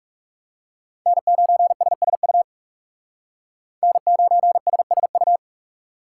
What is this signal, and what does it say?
Morse code, heard as a keyed tone, about 36 wpm, 700 Hz. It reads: N9SSU N9SSU
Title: Speed Racing - USA 1x3 Call Signs 30wpm